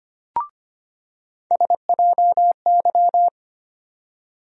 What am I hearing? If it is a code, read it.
Morse code, heard as a keyed tone, about 25 wpm, 700 Hz. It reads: SJY
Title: USA Call Sign Postfixes - 3 Letters 25wpm